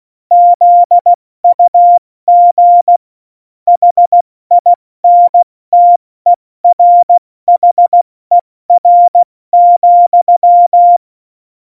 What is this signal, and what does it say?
Morse code, heard as a keyed tone, about 16 wpm, 700 Hz. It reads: ZUG HINTERHER,
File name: Morsekurs / Witze / Witz 11 Juli 17, Tempo 70 F80